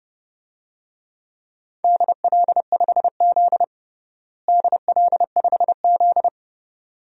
Morse code, heard as a keyed tone, about 32 wpm, 700 Hz. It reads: DL5Z DL5Z
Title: DXCC-100 Call Signs - 4 Characters - Repeated 2x - 30wpm